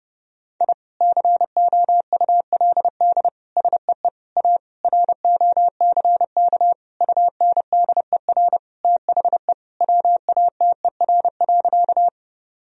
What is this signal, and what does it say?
Morse code, heard as a keyed tone, about 30 wpm, 700 Hz. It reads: I COULD SEE A ROCK UNDER THE WATER.
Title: Sentences from Top 1000 Words - Review Part 2 - 30wpm